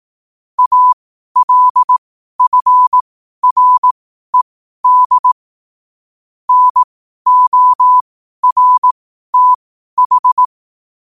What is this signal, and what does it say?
Morse code, heard as a keyed tone, about 18 wpm, spaced as for 12 wpm, 1000 Hz. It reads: ALFRED NORTH
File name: qotdfiles / 12wpm